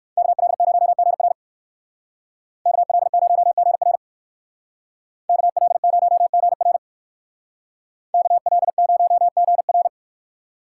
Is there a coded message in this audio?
Yes: Morse code from 0.2 to 9.9 s, heard as a keyed tone, about 63 wpm, 700 Hz. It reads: KL0GR KL0GR KL0GR KL0GR